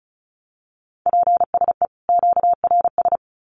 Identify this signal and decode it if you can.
Morse code, heard as a keyed tone, about 34 wpm, 700 Hz. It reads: PSE QRS
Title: Common QSO Phrases 35wpm